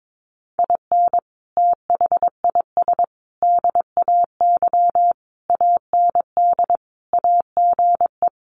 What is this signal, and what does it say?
Morse code, heard as a keyed tone, about 22 wpm, 700 Hz. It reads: IN THIS DAY AND AGE
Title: Sets of 5 Words Encore 22wpm